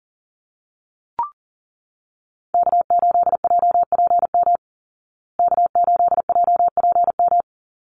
Morse code, heard as a keyed tone, about 40 wpm, 700 Hz. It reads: K8JPM K8JPM
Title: Call Signs Repeated 2x - 40wpm